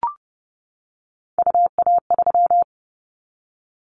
Morse code, heard as a keyed tone, about 30 wpm, 700 Hz. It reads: UA3
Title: Callsign Prefixes for Top 100 DXCC - 30wpm